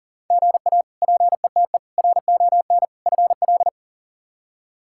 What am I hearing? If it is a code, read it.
Morse code, heard as a keyed tone, about 37 wpm, 700 Hz. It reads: GA PETE RON FL
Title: K1USN SST QSO Practice - 40wpm